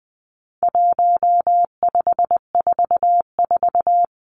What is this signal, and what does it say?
Morse code, heard as a keyed tone, about 20 wpm, 700 Hz. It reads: 1544
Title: Numbers - 4-Digit 20wpm